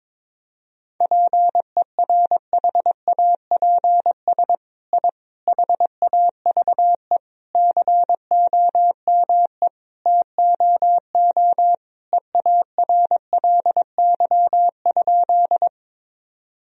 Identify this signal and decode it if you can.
Morse code, heard as a keyed tone, about 22 wpm, 700 Hz. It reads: PERHAPS I HAVE COME TOO EARLY?
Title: Sentences from Top 1000 Words - Review Part 2 - 22wpm